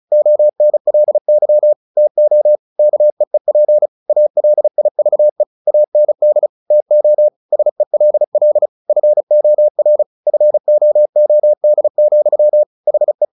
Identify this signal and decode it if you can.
Morse code, heard as a keyed tone, about 34 wpm, 600 Hz. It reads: ONLY TO KEEP ALIVE AND TO SELL FOR FOOD, HE
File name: the old man and the sea 2 35wpm